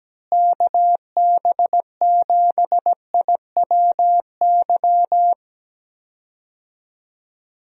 Morse code, heard as a keyed tone, about 17 wpm, 700 Hz. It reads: KB7IWY
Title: USA Call Signs - 6 Characters - Repeated 3x - 17wpm